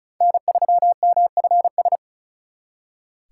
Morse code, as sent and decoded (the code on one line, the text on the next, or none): -. ...-- -- ..-. ...
N3MFS